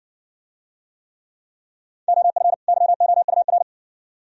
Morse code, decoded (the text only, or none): OF YOUR